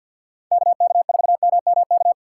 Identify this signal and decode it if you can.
Morse code, heard as a keyed tone, about 49 wpm, 700 Hz. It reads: K K 4 M M K
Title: USA Call Signs - 6 Characters - 6x Character Spacing - 50wpm